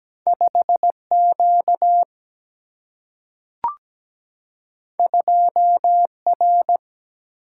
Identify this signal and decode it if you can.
Morse code, heard as a keyed tone, about 17 wpm, 700 Hz. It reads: 5Q 2R
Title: Single-Letter Single-Number 17wpm